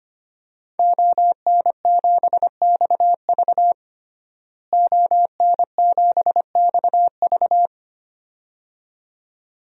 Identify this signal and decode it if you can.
Morse code, heard as a keyed tone, about 25 wpm, 700 Hz. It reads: ON7XV ON7XV
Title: DXCC-100 Call Signs - 5 Characters - Repeated 2x - 25wpm